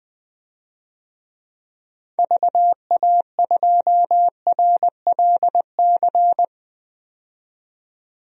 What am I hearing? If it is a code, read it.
Morse code, heard as a keyed tone, about 20 wpm, 700 Hz. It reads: VA2RLC
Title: DXCC-100 Call Signs - 6 Characters - 20wpm